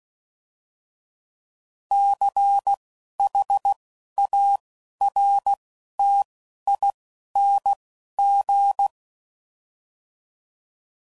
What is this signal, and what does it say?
Morse code, heard as a keyed tone, about 16 wpm, 800 Hz. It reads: C H A R T I N G